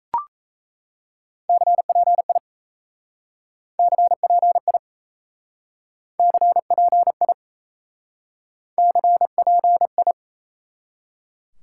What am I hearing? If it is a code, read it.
Morse code, heard as a keyed tone, about 36 wpm, 700 Hz. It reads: CPI CPI CPI CPI